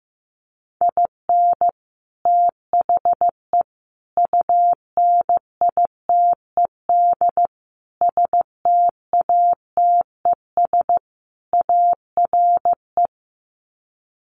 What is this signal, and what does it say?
Morse code, heard as a keyed tone, about 15 wpm, 700 Hz. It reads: IN THE UNITED STATES ARE
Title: Sets of 5 Words Encore 15wpm